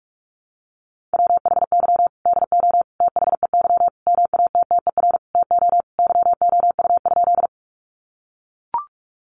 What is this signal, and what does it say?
Morse code, heard as a keyed tone, about 44 wpm, 700 Hz. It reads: WHY DO THEY MATTER TO YOU?
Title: Sentences from Top 700 Words 45wpm